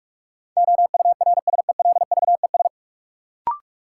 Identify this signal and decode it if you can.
Morse code, heard as a keyed tone, about 46 wpm, 700 Hz. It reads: OURSELVES
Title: Sentences from Top 1600 Words - New Word Drill Alt - 45wpm